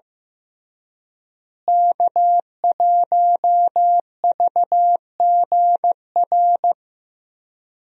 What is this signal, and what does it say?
Morse code, heard as a keyed tone, about 15 wpm, 700 Hz. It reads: K1VGR
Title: USA Call Signs - 5 Characters - Repeated 2x - 15wpm